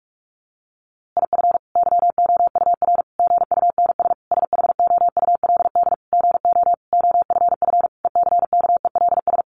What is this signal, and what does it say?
Morse code, heard as a keyed tone, about 44 wpm, 700 Hz. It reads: IF YOUR GUNS SHOULD GO OFF ECKELS
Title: A Sound Of Thunder 45wpm